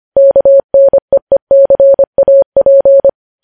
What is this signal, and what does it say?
Morse code, heard as a keyed tone, about 25 wpm, 550 Hz. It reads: KNEECAP
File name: podcast / News headlines-1x-25x25